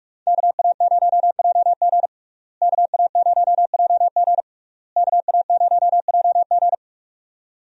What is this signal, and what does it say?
Morse code, heard as a keyed tone, about 46 wpm, 700 Hz. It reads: KA0JG KA0JG KA0JG
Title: USA Call Signs - 5 Characters - Repeated 3x - 45wpm